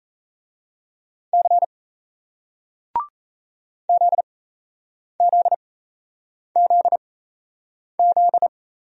Morse code, heard as a keyed tone, about 38 wpm, spaced as for 8 wpm, 700 Hz. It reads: C ZZZZ